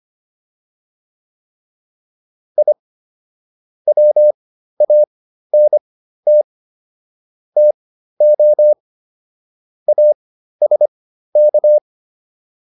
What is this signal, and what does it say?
Morse code, heard as a keyed tone, about 25 wpm, spaced as for 13 wpm, 600 Hz. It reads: I WANT TO ASK